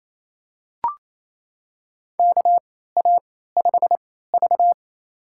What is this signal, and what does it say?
Morse code, heard as a keyed tone, about 28 wpm, spaced as for 16 wpm, 700 Hz. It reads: KA5V
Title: USA Call Signs - 3x Character Spacing - 28wpm